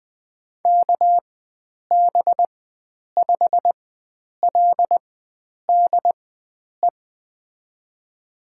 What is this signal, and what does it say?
Morse code, heard as a keyed tone, about 20 wpm, spaced as for 9 wpm, 700 Hz. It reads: KB5LDE
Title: USA Call Signs - 4x Character Spacing - 20wpm